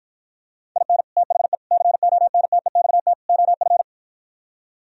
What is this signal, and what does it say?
Morse code, heard as a keyed tone, about 52 wpm, 700 Hz. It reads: IN THE CONTEXT OF